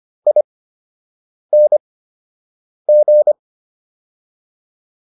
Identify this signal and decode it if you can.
Morse code, heard as a keyed tone, about 25 wpm, spaced as for 7 wpm, 600 Hz. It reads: ING